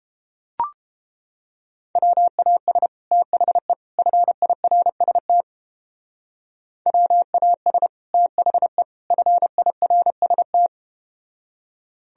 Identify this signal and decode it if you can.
Morse code, heard as a keyed tone, about 31 wpm, 700 Hz. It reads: WAS THE FIRST WAS THE FIRST